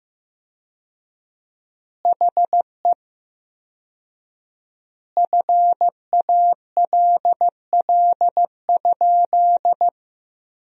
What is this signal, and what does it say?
Morse code, heard as a keyed tone, about 15 wpm, 700 Hz. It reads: HE FALL?
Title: 4x Word-Spacing - Sentences from Top 500 Words - 15wpm